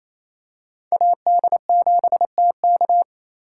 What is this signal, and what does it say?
Morse code, heard as a keyed tone, about 28 wpm, 700 Hz. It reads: AD7TK
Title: USA Call Signs - 2x Character Spacing - 28wpm